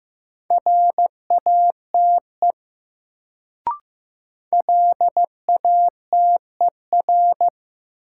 Morse code, heard as a keyed tone, about 15 wpm, 700 Hz. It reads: RATE LATER
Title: Sentences from Top 700 Words - New Word Drill - 15wpm